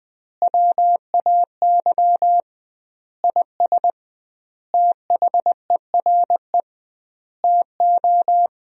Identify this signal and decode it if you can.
Morse code, heard as a keyed tone, about 20 wpm, 700 Hz. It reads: WAY IS THERE TO
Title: 2x Word-Spacing - Sentences from Top 100 Words - 20wpm